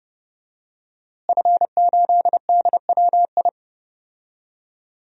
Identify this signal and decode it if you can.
Morse code, heard as a keyed tone, about 30 wpm, 700 Hz. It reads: F8DWI
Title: DXCC-100 Call Signs - 5 Characters - Repeated 3x - 30wpm